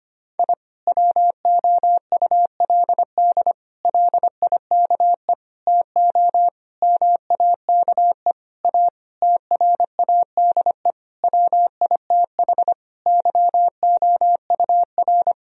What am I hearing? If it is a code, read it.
Morse code, heard as a keyed tone, about 25 wpm, 700 Hz. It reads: I WOULD LIKE TO MAKE A TRADE WITH YOUR
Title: Sentences from Top 1000 Words - Review Part 4 - 25wpm